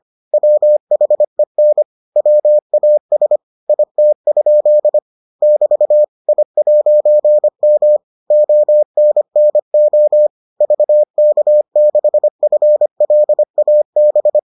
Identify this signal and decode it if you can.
Morse code, heard as a keyed tone, about 25 wpm, 600 Hz. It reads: WHEN WAS IT? <BT> I'M ONNO VK6FLAB